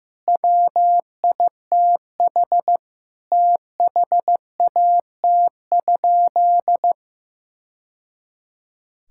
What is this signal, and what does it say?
Morse code, heard as a keyed tone, about 16 wpm, 700 Hz. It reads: WITH THAT?
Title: Sentences from Top 1000 Words - Review Part 6 - 15wpm